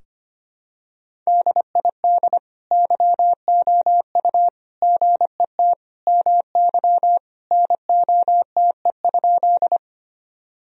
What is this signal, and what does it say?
Morse code, heard as a keyed tone, about 25 wpm, 700 Hz. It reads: DID YOU GET MY NOTE?
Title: Sentences from Top 1000 Words - Review Part 2 - 25wpm